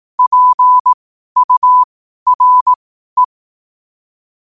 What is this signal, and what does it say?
Morse code, heard as a keyed tone, about 18 wpm, spaced as for 12 wpm, 1000 Hz. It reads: PURE
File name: 2025-11-08-12wpm